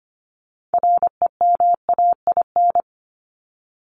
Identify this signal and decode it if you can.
Morse code, heard as a keyed tone, about 25 wpm, 700 Hz. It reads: REMAIN
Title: ED Suffix - Top 100 Words with Base Words 25wpm